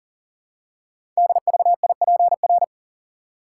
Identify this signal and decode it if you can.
Morse code, heard as a keyed tone, about 40 wpm, 690 Hz. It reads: DVIPR
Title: Code Groups - 5 Letters - 40wpm